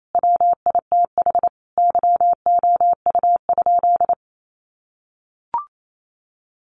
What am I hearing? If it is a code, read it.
Morse code, heard as a keyed tone, about 28 wpm, 700 Hz. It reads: WITH YOU?
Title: Sentences from Top 100 Words 28wpm